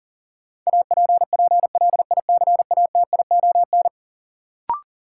Morse code, heard as a keyed tone, about 40 wpm, 700 Hz. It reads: APPLICATION
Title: Sentences from Top 1200 Words - New Word Drill - 40wpm